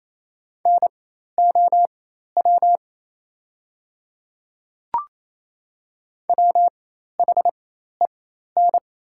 Morse code, heard as a keyed tone, about 28 wpm, spaced as for 13 wpm, 700 Hz. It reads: NOW WHEN
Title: 4x Character Spacing - Top 100 Words - 28wpm